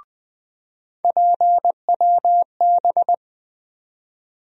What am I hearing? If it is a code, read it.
Morse code, heard as a keyed tone, about 20 wpm, 700 Hz. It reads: PWB
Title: USA Call Sign Postfixes - 3 Letters 20wpm